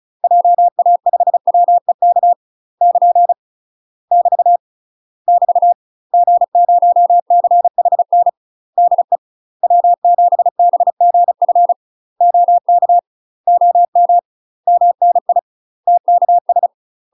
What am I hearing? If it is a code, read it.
Morse code, heard as a keyed tone, about 36 wpm, 700 Hz. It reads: JA5WEK <KN> <BT> <BT> G0CHN DE W7BGF OK OM MNI TKS
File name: UK MASTER QSOs TRAINING / 35WPM / UK MASTER QSOs 0036